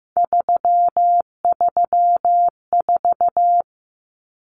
Morse code, heard as a keyed tone, about 15 wpm, 700 Hz. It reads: 334
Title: Numbers - 3-Digit 15wpm